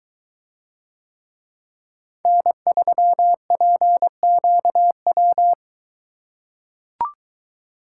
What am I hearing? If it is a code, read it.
Morse code, heard as a keyed tone, about 22 wpm, 700 Hz. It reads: N3PQW